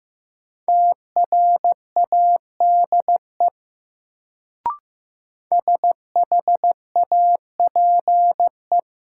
Morse code, heard as a keyed tone, about 15 wpm, 700 Hz. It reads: TRADE SHAPE
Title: Sets of 5 Words Encore - New Word Drill - 15wpm